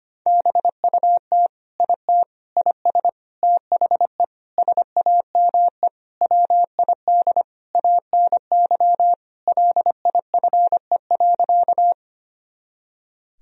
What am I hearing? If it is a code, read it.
Morse code, heard as a keyed tone, about 25 wpm, 700 Hz. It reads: BUT IT IS THE SAME WID ANY LIFE.
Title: Sentences from Top 300 Words and Abbreviations 25wpm